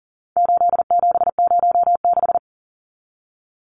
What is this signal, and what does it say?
Morse code, heard as a keyed tone, about 40 wpm, 700 Hz. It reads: 8706
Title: Numbers - 4-Digit 40wpm